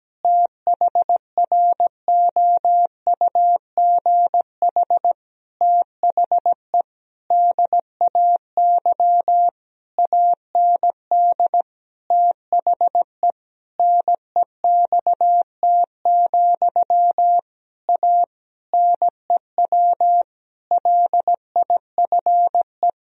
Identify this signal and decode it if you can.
Morse code, heard as a keyed tone, about 17 wpm, 700 Hz. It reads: THROUGH THE DAY AND THE NEXT, A NEW LIFE
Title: Sentences from Top 1000 Words - Review Part 1 - 17wpm